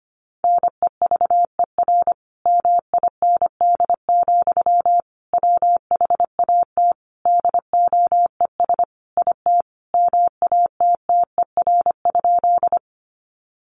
Morse code, heard as a keyed tone, about 25 wpm, 700 Hz. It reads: NEVER MIND, WHAT DOES IT MATTER?
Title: Sentences from Top 700 Words 25wpm